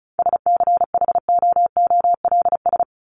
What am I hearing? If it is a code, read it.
Morse code, heard as a keyed tone, about 34 wpm, 700 Hz. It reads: SCHOOLS
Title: ES:S Suffix - Top 100 Words with Base Words 35wpm